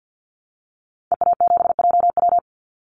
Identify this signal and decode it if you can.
Morse code, heard as a keyed tone, about 50 wpm, 700 Hz. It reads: EA7JW
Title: Call Signs Repeated 2x - 50wpm